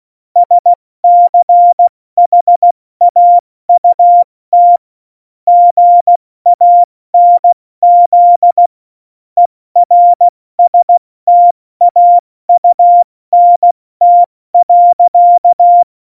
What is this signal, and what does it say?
Morse code, heard as a keyed tone, about 16 wpm, 700 Hz. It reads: SCHAUT GANZ ERSTAUNT.
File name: Morsekurs / Witze / Witz 06. Oktober 2020, 70 F80